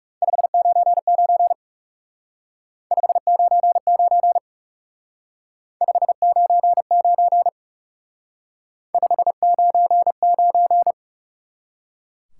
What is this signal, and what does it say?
Morse code, heard as a keyed tone, about 36 wpm, 700 Hz. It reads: 599 599 599 599
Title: Speed-Racing Top 90 QSO Elements 30wpm